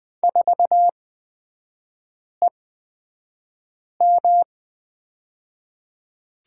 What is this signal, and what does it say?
Morse code, heard as a keyed tone, about 20 wpm, spaced as for 5 wpm, 700 Hz. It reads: 4EM
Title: 224 Lesson 36e - BT - Three Characters 20wpm